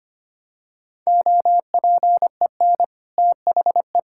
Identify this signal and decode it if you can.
Morse code, heard as a keyed tone, about 25 wpm, 700 Hz. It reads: OPEN THE
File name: practice / 25wpm / Sentences with 2-Word Phrases - 25wpm